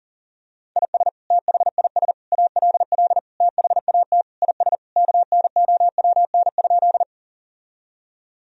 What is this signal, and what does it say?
Morse code, heard as a keyed tone, about 40 wpm, 700 Hz. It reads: IS THIS ALL THAT IS KNOWN?
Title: Sentences from Top 1000 Words - Review Part 3 - 40wpm